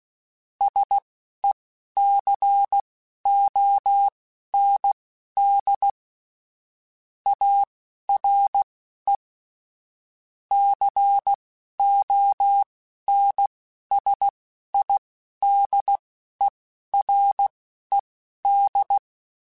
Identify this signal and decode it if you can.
Morse code, heard as a keyed tone, about 16 wpm, spaced as for 11 wpm, 800 Hz. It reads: SECOND ARE CONSIDERED
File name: Technical Subjects / Technical Subjects-mp3s / XC @10